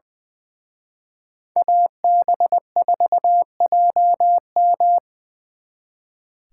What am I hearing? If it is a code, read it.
Morse code, heard as a keyed tone, about 24 wpm, 700 Hz. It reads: AB4JM
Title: Speed Racing - USA 2x2 Call Signs 20wpm